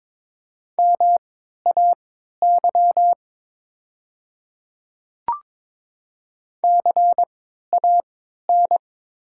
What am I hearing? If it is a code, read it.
Morse code, heard as a keyed tone, about 22 wpm, spaced as for 12 wpm, 700 Hz. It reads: MAY CAN
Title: 3x Character Spacing - Top 100 Words - 22wpm